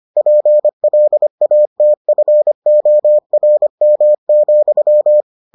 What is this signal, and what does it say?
Morse code, heard as a keyed tone, about 25 wpm, 600 Hz. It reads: PLATFORM,